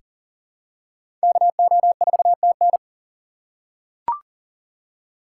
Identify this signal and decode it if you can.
Morse code, heard as a keyed tone, about 40 wpm, 700 Hz. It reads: KO4TN
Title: USA Call Signs - 5 Characters - 3x Character Spacing - 40wpm